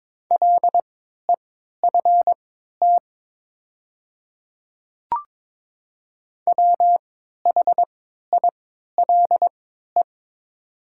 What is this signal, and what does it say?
Morse code, heard as a keyed tone, about 22 wpm, spaced as for 12 wpm, 700 Hz. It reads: LEFT WHILE
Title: 3x Character Spacing - Top 300 Words - 22wpm